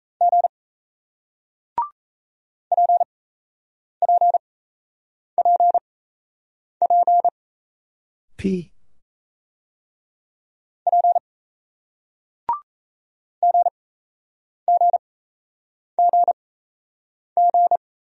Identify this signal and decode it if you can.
Morse code, heard as a keyed tone, about 37 wpm, spaced as for 8 wpm, 700 Hz. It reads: G PPPP P GGGG